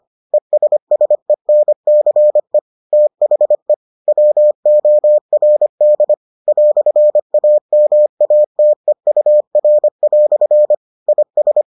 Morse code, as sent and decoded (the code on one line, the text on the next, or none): . ... ... -. -.-. / - .... / .-- --- .-. -.. / .-..-. .- -- .- - . ..- .-. .-..-. / .. ...
ESSENCE THE WORD "AMATEUR" IS